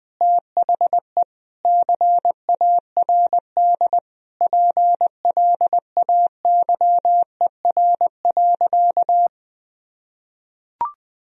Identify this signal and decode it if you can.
Morse code, heard as a keyed tone, about 20 wpm, 700 Hz. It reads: THE CARD PLAYER.
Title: Sentences from Top 1000 Words - Review Part 4 - 20wpm